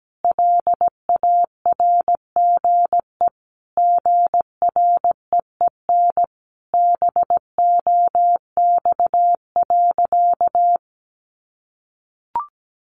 Morse code, heard as a keyed tone, about 17 wpm, 700 Hz. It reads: LARGE GREEN BOX.
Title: Sentences from Top 500 Words 17wpm